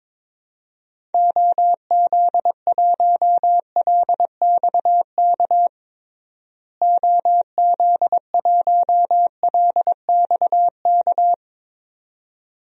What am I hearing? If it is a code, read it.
Morse code, heard as a keyed tone, about 22 wpm, 700 Hz. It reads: OZ1LXK OZ1LXK
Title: DXCC-100 Call Signs - 6 Characters - Repeated 2x - 22wpm